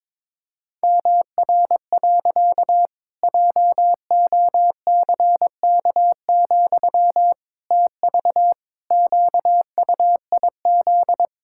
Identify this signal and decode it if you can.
Morse code, heard as a keyed tone, about 22 wpm, 700 Hz. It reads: MR. JOCK, TV QUIZ
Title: Pangram Challenge v2 - 22wpm